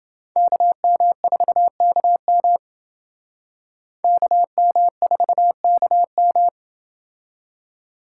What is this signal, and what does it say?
Morse code, heard as a keyed tone, about 28 wpm, 700 Hz. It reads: KM4KM KM4KM